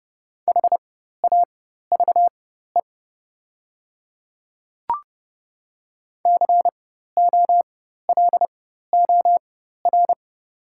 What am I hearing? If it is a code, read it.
Morse code, heard as a keyed tone, about 30 wpm, spaced as for 14 wpm, 700 Hz. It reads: HAVE COLOR
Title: 4x Character Spacing - Top 300 Words - 30wpm